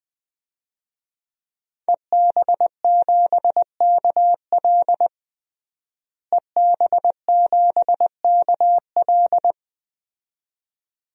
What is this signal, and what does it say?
Morse code, heard as a keyed tone, about 20 wpm, 700 Hz. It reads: EB7KL EB7KL
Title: DXCC-100 Call Signs - 5 Characters - Repeated 2x - 20wpm